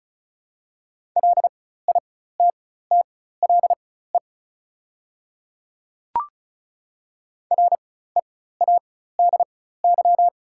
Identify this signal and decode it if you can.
Morse code, heard as a keyed tone, about 34 wpm, spaced as for 16 wpm, 690 Hz. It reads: LITTLE READY
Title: 4x Character Spacing - Top 500 Words - 35wpm